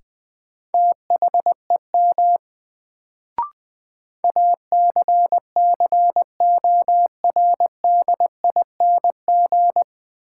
Morse code, heard as a keyed tone, about 20 wpm, 700 Hz. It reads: THEM ACCORDING
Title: Sets of 2 Words - New Word Drill - 20wpm